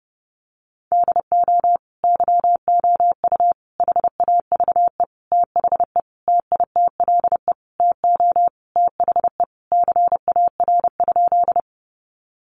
Morse code, heard as a keyed tone, about 28 wpm, 700 Hz. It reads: DO YOU HAVE THE TITLE TO THE CAR?
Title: Sentences from Top 1200 Words 30wpm